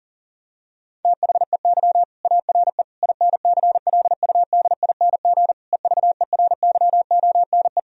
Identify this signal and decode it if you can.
Morse code, heard as a keyed tone, about 40 wpm, 700 Hz. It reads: THEY ARE INCLUDING EVERYONE
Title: Sentences from Top 1000 Words - Review Part 4 - 40wpm